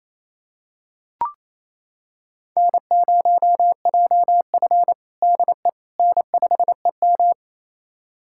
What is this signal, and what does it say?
Morse code, heard as a keyed tone, about 28 wpm, 700 Hz. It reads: N0JF DE N5EM
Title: USA Callsign DE Callsign - 4 Characters - 28wpm